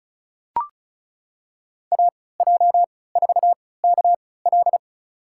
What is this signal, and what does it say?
Morse code, heard as a keyed tone, about 34 wpm, spaced as for 20 wpm, 700 Hz. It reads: AJ4KL
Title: USA Call Signs - 5 Characters - 3x Character Spacing - 35wpm